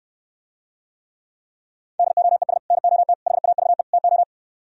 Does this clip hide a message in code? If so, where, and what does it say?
Yes: Morse code from 2.0 to 4.2 s, heard as a keyed tone, about 68 wpm, 700 Hz. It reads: DOES NOT HAVE TO